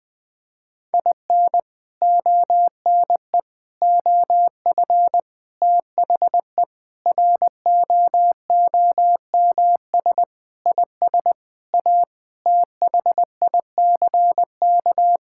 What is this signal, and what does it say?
Morse code, heard as a keyed tone, about 20 wpm, 700 Hz. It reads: IN ONE OF THE ROOMS IS A THICK
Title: Sentences with 4-Word Phrases - 20wpm